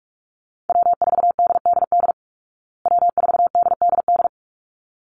Morse code, heard as a keyed tone, about 44 wpm, 700 Hz. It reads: W4DDD W4DDD
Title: Call Signs Repeated 3x - 45wpm